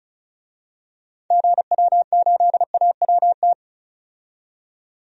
Morse code, heard as a keyed tone, about 34 wpm, 700 Hz. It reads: GW8AWT
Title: DXCC-100 Call Signs - 6 Characters - 35wpm